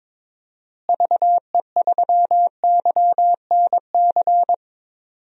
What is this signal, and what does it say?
Morse code, heard as a keyed tone, about 22 wpm, 700 Hz. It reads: VE3YNC
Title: DXCC-100 Call Signs - 6 Characters - Repeated 2x - 22wpm